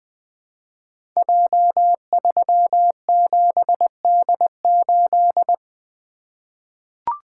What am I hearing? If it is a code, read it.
Morse code, heard as a keyed tone, about 20 wpm, 700 Hz. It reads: J37D8
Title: Code Groups - 5 Letter:Numbers - 20wpm